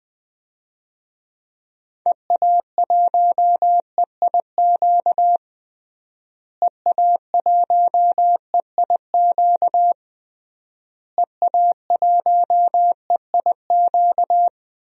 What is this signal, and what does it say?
Morse code, heard as a keyed tone, about 20 wpm, 700 Hz. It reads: EA1EIQ EA1EIQ EA1EIQ
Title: DXCC-100 Call Signs - 6 Characters - Repeated 3x - 20wpm